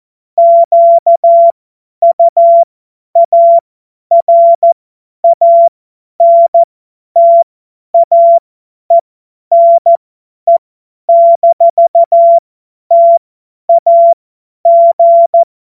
Morse code, heard as a keyed tone, about 14 wpm, spaced as for 10 wpm, 680 Hz. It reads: QUARANTAENE-TAG
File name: Morsekurs / Witze / Witz 06. Juli 2021, 50 F70